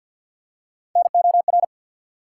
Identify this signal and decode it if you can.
Morse code, heard as a keyed tone, about 53 wpm, 700 Hz. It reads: NOR
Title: Three Letter Words - 50wpm